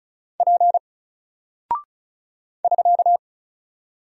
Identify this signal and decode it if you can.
Morse code, heard as a keyed tone, about 36 wpm, 700 Hz. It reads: P <SK>
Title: Single Character (Letter:Number:Punctuation:Prosign) 35wpm